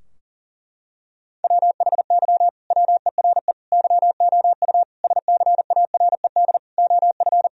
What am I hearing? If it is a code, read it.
Morse code, heard as a keyed tone, about 37 wpm, 700 Hz. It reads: WHY WERE YOU SCARED OF
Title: Sentences from Top 1000 Words - Review Part 4 - 40wpm